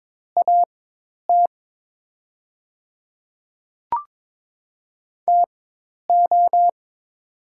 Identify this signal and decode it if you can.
Morse code, heard as a keyed tone, about 22 wpm, spaced as for 10 wpm, 700 Hz. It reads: AT TO
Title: 4x Character Spacing - Top 100 Words - 22wpm